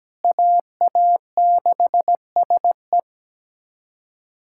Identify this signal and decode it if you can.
Morse code, heard as a keyed tone, about 17 wpm, 700 Hz. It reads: AA6SE
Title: USA Callsign DE Callsign - 5 Characters - 17wpm